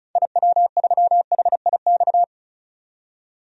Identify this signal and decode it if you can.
Morse code, heard as a keyed tone, about 36 wpm, 700 Hz. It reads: IW3HIX
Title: DXCC-100 Call Signs - 6 Characters - 35wpm